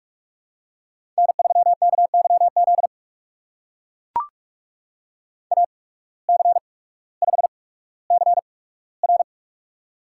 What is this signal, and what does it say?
Morse code, heard as a keyed tone, about 44 wpm, spaced as for 19 wpm, 690 Hz. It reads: N3KYZ AC5CR
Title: USA Call Signs - 5 Characters - 8x Character Spacing - 45wpm